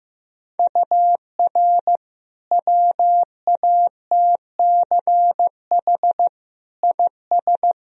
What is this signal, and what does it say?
Morse code, heard as a keyed tone, about 16 wpm, 700 Hz. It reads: UR WATCH IS
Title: Sentences from Top 400 Words and Abbreviations 15wpm